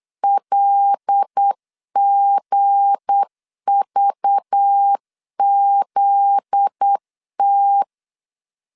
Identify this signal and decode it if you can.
Morse code, heard as a keyed tone, about 8 wpm, 800 Hz. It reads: LGVZT
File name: LEZIONI / 10 SIGNS / 10B INT 8wpm 3534